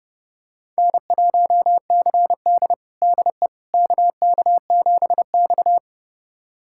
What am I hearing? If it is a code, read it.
Morse code, heard as a keyed tone, about 31 wpm, 700 Hz. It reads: N1CD DE KK7X
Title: USA Callsign DE Callsign - 4 Characters - 30wpm